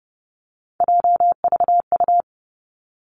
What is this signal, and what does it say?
Morse code, heard as a keyed tone, about 30 wpm, 700 Hz. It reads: JVU
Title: Three Challenging Letter-Number Combinations 30wpm